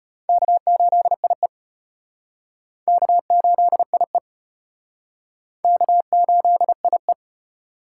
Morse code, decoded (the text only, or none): K8IE K8IE K8IE